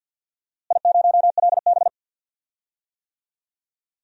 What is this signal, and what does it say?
Morse code, heard as a keyed tone, about 53 wpm, 700 Hz. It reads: I0LB
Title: DXCC-100 Call Signs - 4 Characters - Repeated 2x - 50wpm